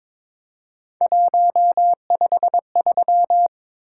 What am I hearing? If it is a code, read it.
Morse code, heard as a keyed tone, about 22 wpm, 700 Hz. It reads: 153
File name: practice / 22wpm / Three Letter-Number Combinations - 22wpm